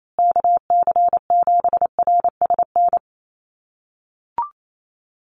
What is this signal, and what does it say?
Morse code, heard as a keyed tone, about 28 wpm, 700 Hz. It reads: KC7RSN
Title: USA Call Signs Repeated 2x 28wpm